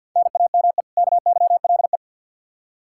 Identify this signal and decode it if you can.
Morse code, heard as a keyed tone, about 46 wpm, 700 Hz. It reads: NAME KYLE
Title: Practice QSO - Exchange 1 - Beginning - 50wpm